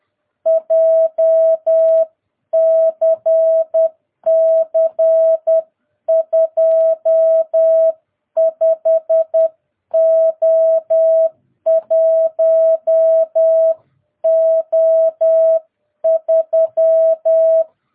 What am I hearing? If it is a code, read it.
Morse code, heard as a keyed tone, about 10 wpm, 660 Hz. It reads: JCC 25O1O3
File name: jh8cbh / 2022cw